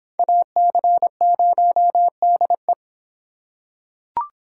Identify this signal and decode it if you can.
Morse code, heard as a keyed tone, about 24 wpm, 700 Hz. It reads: AC0DE